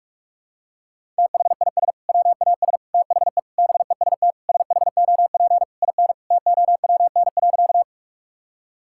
Morse code, heard as a keyed tone, about 41 wpm, 700 Hz. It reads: THIS WAS THE BEST SHOP IN TOWN.
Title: Sentences from Top 1600 Words 45wpm